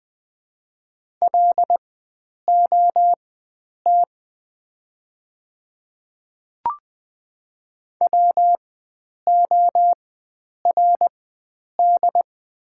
Morse code, heard as a keyed tone, about 20 wpm, spaced as for 9 wpm, 700 Hz. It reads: LOT WORD
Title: 4x Character Spacing - Top 500 Words - 20wpm